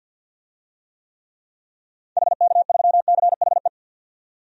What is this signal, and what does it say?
Morse code, heard as a keyed tone, about 50 wpm, 700 Hz. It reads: HK3CHE
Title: DXCC-100 Call Signs - 6 Characters - 50wpm